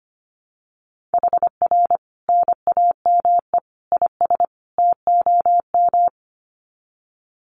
Morse code, heard as a keyed tone, about 25 wpm, 700 Hz. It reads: HR NAME IS TOM
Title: Common QSO Phrases 25wpm